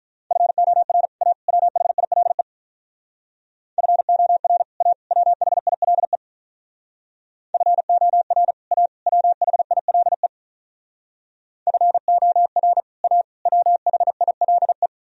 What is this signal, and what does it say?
Morse code, heard as a keyed tone, about 44 wpm, 700 Hz. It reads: FOR A WHILE FOR A WHILE FOR A WHILE FOR A WHILE